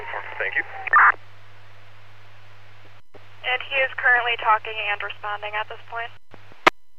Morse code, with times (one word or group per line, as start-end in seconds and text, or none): none